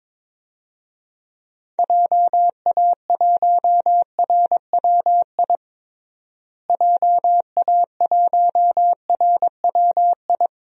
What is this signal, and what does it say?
Morse code, heard as a keyed tone, about 22 wpm, 700 Hz. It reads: JA1RWI JA1RWI
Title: USA Call Signs - 6 Characters - Repeated 2x - 22wpm